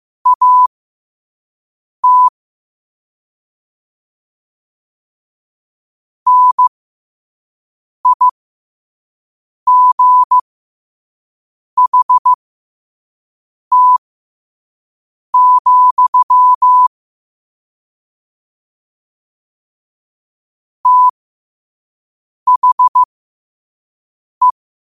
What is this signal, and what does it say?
Morse code, heard as a keyed tone, about 15 wpm, spaced as for 5 wpm, 1000 Hz. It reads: AT NIGHT, THE
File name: qotdfiles / 5wpm